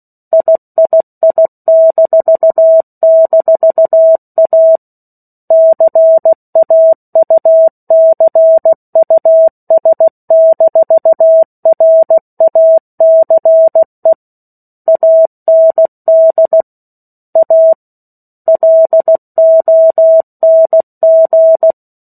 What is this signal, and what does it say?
Morse code, heard as a keyed tone, about 16 wpm, 650 Hz. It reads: III--A CAUCUS-RACE AND A LONG
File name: 03-Chapter-03 16 WPM